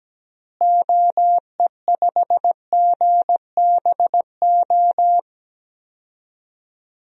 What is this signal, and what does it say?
Morse code, heard as a keyed tone, about 17 wpm, 700 Hz. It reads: OE5GBO
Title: DXCC-100 Call Signs - 6 Characters - Repeated 2x - 17wpm